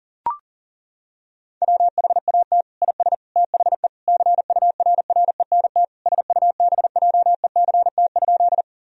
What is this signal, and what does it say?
Morse code, heard as a keyed tone, about 40 wpm, 700 Hz. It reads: WHAT IS THE CURRENT SUBJECT?
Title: Sentences from Top 1000 Words - Review Part 4 - 40wpm